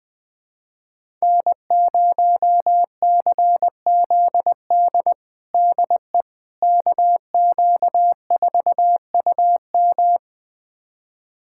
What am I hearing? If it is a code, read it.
Morse code, heard as a keyed tone, about 20 wpm, 700 Hz. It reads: N0CZD DE KQ4UM
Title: USA Callsign DE Callsign - 5 Characters - 20wpm